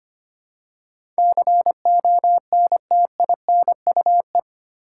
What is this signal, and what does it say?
Morse code, heard as a keyed tone, about 25 wpm, 700 Hz. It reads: CONTINUE
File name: 8 Letter Words - First 3 Letters Given - 25wpm